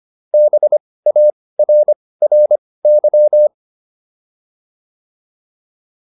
Morse code, heard as a keyed tone, about 25 wpm, spaced as for 18 wpm, 600 Hz. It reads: BARRY